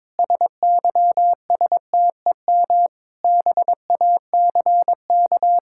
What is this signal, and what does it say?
Morse code, heard as a keyed tone, about 22 wpm, 700 Hz. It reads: SYSTEM BACK
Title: Sentences from Top 1000 Words - Review Part 5 - 22wpm